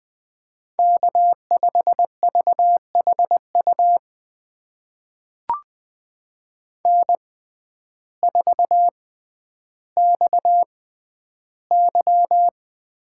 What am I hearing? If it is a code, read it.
Morse code, heard as a keyed tone, about 20 wpm, spaced as for 11 wpm, 700 Hz. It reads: K5VHU N4XY
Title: USA Call Signs - 5 Characters - 6x Character Spacing - 20wpm